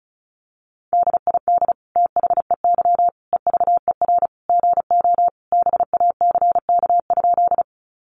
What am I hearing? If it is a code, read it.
Morse code, heard as a keyed tone, about 34 wpm, 700 Hz. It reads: DID THEY EVER GO BACK?
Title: Sentences from Top 400 Words 35wpm